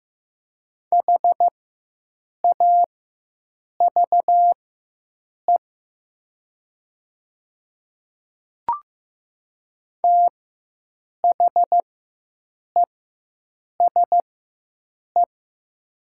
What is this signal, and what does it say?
Morse code, heard as a keyed tone, about 15 wpm, spaced as for 7 wpm, 700 Hz. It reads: HAVE THESE
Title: 4x Character Spacing - Top 100 Words - 15wpm